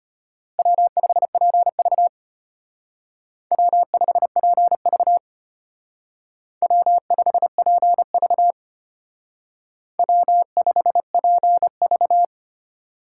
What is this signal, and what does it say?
Morse code, heard as a keyed tone, about 32 wpm, 700 Hz. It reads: W5PV W5PV W5PV W5PV